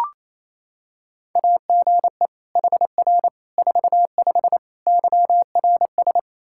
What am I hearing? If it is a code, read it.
Morse code, heard as a keyed tone, about 28 wpm, 700 Hz. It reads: AGE HR 45 YRS
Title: Practice QSO - Exchange 3 - More Info - 28wpm